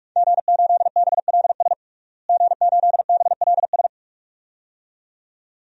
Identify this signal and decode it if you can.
Morse code, heard as a keyed tone, about 46 wpm, 700 Hz. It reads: G8BLS G8BLS
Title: DXCC-100 Call Signs - 5 Characters - Repeated 2x - 45wpm